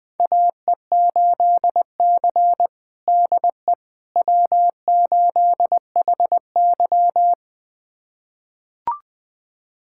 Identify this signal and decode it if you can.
Morse code, heard as a keyed tone, about 20 wpm, 700 Hz. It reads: AE8C DE W8HY
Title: USA Callsign DE Callsign - 4 Characters - 20wpm